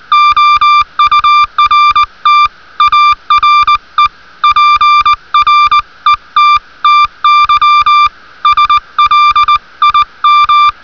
Morse code, heard as a keyed tone, about 20 wpm, 1180 Hz. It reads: OURT ARE PRETTY SLIM